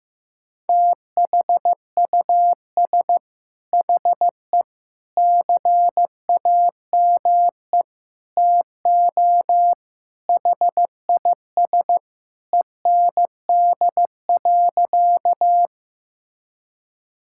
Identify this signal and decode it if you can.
Morse code, heard as a keyed tone, about 15 wpm, 700 Hz. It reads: THUS HE CAME TO HIS END.
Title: Sentences from Top 1000 Words - Review Part 5 - 15wpm